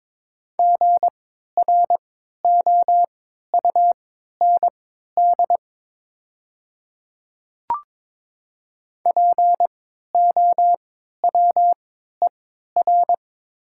Morse code, heard as a keyed tone, about 22 wpm, spaced as for 13 wpm, 700 Hz. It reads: GROUND POWER
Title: 3x Character Spacing - Top 500 Words - 22wpm